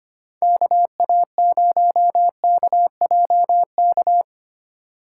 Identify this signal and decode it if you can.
Morse code, heard as a keyed tone, about 25 wpm, 700 Hz. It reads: KA0KJK
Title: USA Call Signs - 6 Characters - 4x Character Spacing - 25wpm